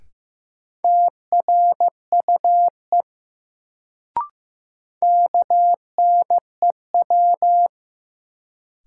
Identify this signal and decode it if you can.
Morse code, heard as a keyed tone, about 15 wpm, 700 Hz. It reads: TRUE KNEW
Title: Sentences from Top 400 Words - New Word Drill - 15wpm